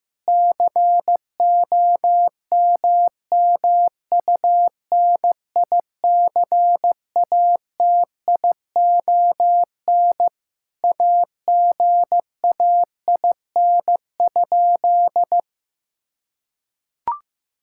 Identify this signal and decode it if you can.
Morse code, heard as a keyed tone, about 15 wpm, 700 Hz. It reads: COMMUNICATION AGAIN?
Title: Sentences from Top 1000 Words - Review Part 6 - 15wpm